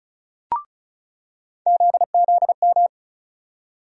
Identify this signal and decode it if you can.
Morse code, heard as a keyed tone, about 36 wpm, 700 Hz. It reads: ZZM
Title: USA Call Sign Postfixes - 3 Letters 35wpm